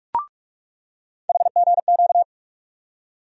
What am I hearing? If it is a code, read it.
Morse code, heard as a keyed tone, about 43 wpm, 700 Hz. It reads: HGQ
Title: USA Call Sign Postfixes - 3 Letters 45wpm